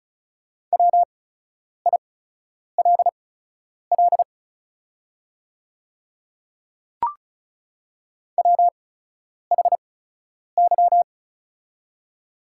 Morse code, heard as a keyed tone, about 36 wpm, spaced as for 10 wpm, 690 Hz. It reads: WILL WHY
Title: 8x Character Spacing - Top 300 Words - 35wpm